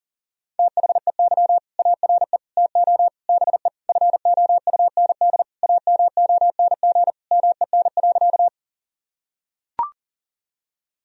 Morse code, heard as a keyed tone, about 37 wpm, 700 Hz. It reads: THEY ARE TO BE FOUND AMONG MEN.
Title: Sentences from Top 1000 Words - Review Part 3 - 40wpm